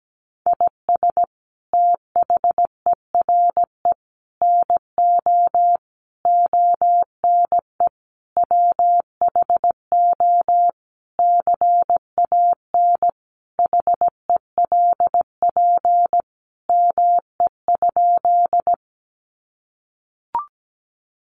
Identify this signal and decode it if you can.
Morse code, heard as a keyed tone, about 17 wpm, 700 Hz. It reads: IS THERE NO ONE WHO CAN HELP ME?
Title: Sentences from Top 200 Words 17wpm